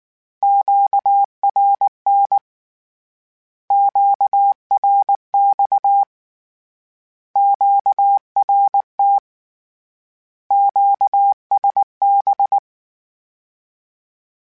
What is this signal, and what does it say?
Morse code, heard as a keyed tone, about 19 wpm, 800 Hz. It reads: QRN QRX QRT QSB
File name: Misc / Misc-mp3s / Words @18